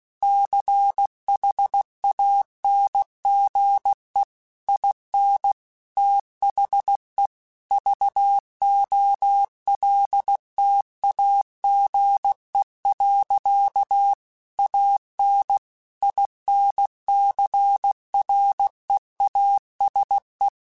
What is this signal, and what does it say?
Morse code, heard as a keyed tone, about 16 wpm, 800 Hz. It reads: CHANGE IN THE VOLTAGE. AN INCREASE
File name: Technical Subjects / Technical Subjects-mp3s / XC @15